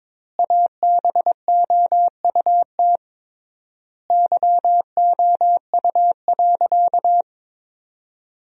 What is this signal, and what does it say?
Morse code, heard as a keyed tone, about 22 wpm, 700 Hz. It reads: ABOUT YOU.
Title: 3x Word-Spacing - Sentences from Top 400 Words - 22wpm